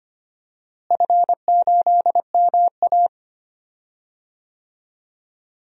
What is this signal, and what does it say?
Morse code, heard as a keyed tone, about 25 wpm, 700 Hz. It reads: F8MA
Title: DXCC-100 Call Signs - 4 Characters - Repeated 2x - 25wpm